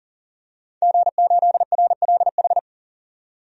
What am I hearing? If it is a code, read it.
Morse code, heard as a keyed tone, about 40 wpm, 700 Hz. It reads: G8RLH
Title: DXCC-100 Call Signs - 5 Characters - 40wpm